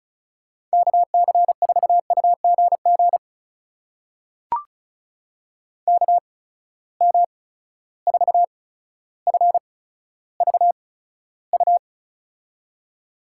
Morse code, heard as a keyed tone, about 36 wpm, spaced as for 15 wpm, 700 Hz. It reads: KC4UGG KM4FVU
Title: USA Call Signs - 6 Characters - 8x Character Spacing - 35wpm